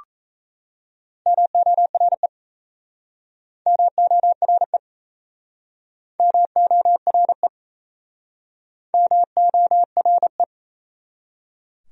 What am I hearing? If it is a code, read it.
Morse code, heard as a keyed tone, about 38 wpm, 700 Hz. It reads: MORE MORE MORE MORE